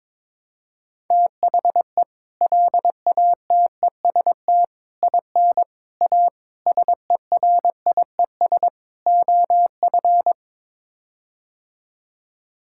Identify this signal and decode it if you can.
Morse code, heard as a keyed tone, about 22 wpm, 700 Hz. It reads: THE LATEST IN A SERIES OF
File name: practice / 22wpm / Sets of 6 Words - 22wpm